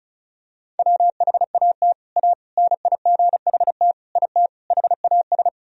Morse code, heard as a keyed tone, about 36 wpm, 700 Hz. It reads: WHAT A NIGHT IT HAS
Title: Sentences from Top 300 Words and Abbreviations 35wpm